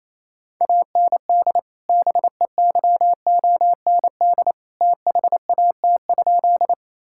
Morse code, heard as a keyed tone, about 28 wpm, 700 Hz. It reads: AND BEYOND THAT?
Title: Sentences from Top 1000 Words - Review Part 6 - 28wpm